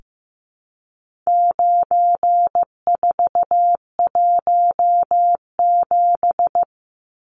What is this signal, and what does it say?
Morse code, heard as a keyed tone, about 15 wpm, 700 Hz. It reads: 9417
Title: Numbers - 4-Digit 15wpm